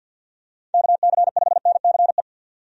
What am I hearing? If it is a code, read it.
Morse code, heard as a keyed tone, about 52 wpm, 700 Hz. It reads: KC5NCE
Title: USA Call Signs - 6 Characters - 50wpm